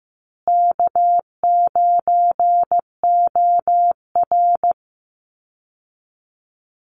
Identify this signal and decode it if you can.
Morse code, heard as a keyed tone, about 15 wpm, 700 Hz. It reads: K9OR
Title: Call Signs Repeated 2x - 15wpm